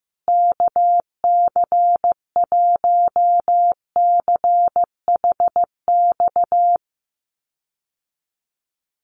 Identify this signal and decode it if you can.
Morse code, heard as a keyed tone, about 15 wpm, 700 Hz. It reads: KC1CHX
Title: USA Call Signs Repeated 3x 15wpm